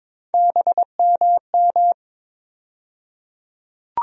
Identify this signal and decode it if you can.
Morse code, heard as a keyed tone, about 22 wpm, 700 Hz. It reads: BMM
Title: USA Call Sign Postfixes - 3 Letters 22wpm